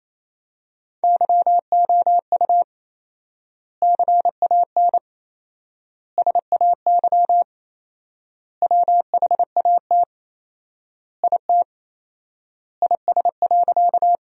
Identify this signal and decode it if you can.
Morse code, heard as a keyed tone, about 28 wpm, 700 Hz. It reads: YOU CAN SAY WHAT IT IS.
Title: 4x Word-Spacing - Sentences from Top 200 Words - 28wpm